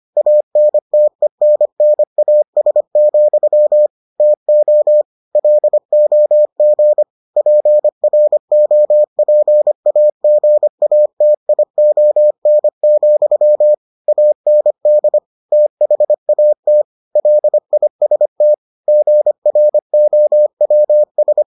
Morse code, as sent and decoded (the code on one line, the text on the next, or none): .- -. - . -. -. .- ... --..-- / - --- / .-.. --- --. / .--. .-. --- .--. .- --. .- - .. --- -. --..-- / .- -. -.. / - .... .- - / .-.. .. ... - / --. .-. --- .-- ...
ANTENNAS, TO LOG PROPAGATION, AND THAT LIST GROWS